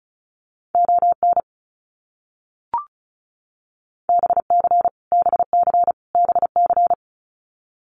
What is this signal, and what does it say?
Morse code, heard as a keyed tone, about 34 wpm, 700 Hz. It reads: ON BC BC BC
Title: Canadian Provinces and Territories - Repeated 3x - 35wpm